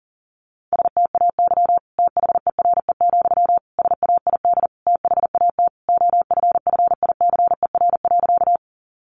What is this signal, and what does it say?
Morse code, heard as a keyed tone, about 40 wpm, 700 Hz. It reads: STAY THERE, SAID THAT OFFICER.
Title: Sentences from Top 1200 Words 40wpm